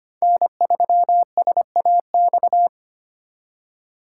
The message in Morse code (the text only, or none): N3SAX